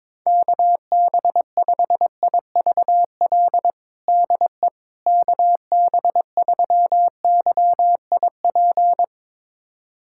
Morse code, heard as a keyed tone, about 22 wpm, 700 Hz. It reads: KB5IVL DE KB3YIP
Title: USA Callsign DE Callsign - 6 Characters - 22wpm